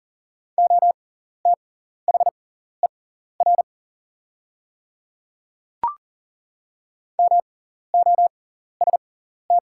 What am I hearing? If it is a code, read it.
Morse code, heard as a keyed tone, about 38 wpm, spaced as for 14 wpm, 700 Hz. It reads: OTHER MOST
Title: 6x Character Spacing - Top 100 Words - 40wpm